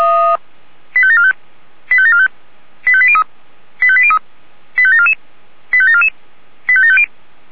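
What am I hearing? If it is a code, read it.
Five-tone selective call (ZVEI), heard as quick runs of five tones, tones of about 70 ms. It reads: 85745 85745 85803 85803 85744 85744 85750
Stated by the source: ZVEI five-tone call received by SDR.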